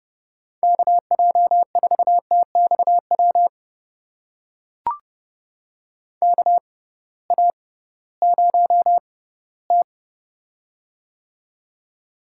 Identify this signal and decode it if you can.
Morse code, heard as a keyed tone, about 30 wpm, spaced as for 18 wpm, 700 Hz. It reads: KJ4TXW KA0T
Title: USA Call Signs - 6x Character Spacing - 30wpm